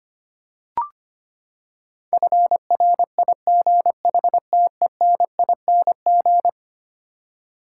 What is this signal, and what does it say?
Morse code, heard as a keyed tone, about 25 wpm, 700 Hz. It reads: FRIGHTENING
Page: morse-code-ninja-practice-25wpm